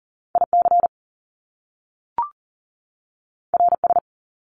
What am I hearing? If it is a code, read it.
Morse code, heard as a keyed tone, about 42 wpm, 700 Hz. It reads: IC RS
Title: Top Two Letter Combinations in Words 40wpm